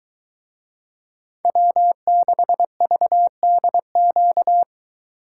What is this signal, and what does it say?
Morse code, heard as a keyed tone, about 23 wpm, 700 Hz. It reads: W6VDQ